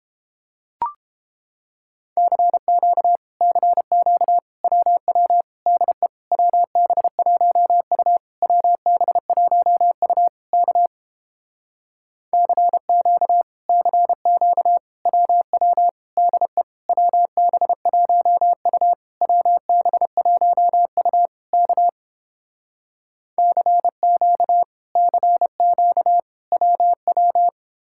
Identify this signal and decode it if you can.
Morse code, heard as a keyed tone, about 31 wpm, 700 Hz. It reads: CQ CQ WW DE WB1U WB1U K CQ CQ WW DE WB1U WB1U K CQ CQ WW